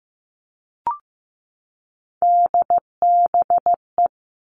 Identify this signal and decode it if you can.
Morse code, heard as a keyed tone, about 15 wpm, 700 Hz. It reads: DBE
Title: All Cut Numbers - 3-Digit 15wpm